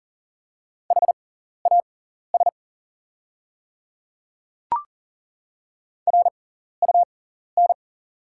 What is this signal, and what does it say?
Morse code, heard as a keyed tone, about 40 wpm, spaced as for 14 wpm, 700 Hz. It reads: HAS RUN
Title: 6x Character Spacing - Top 300 Words - 40wpm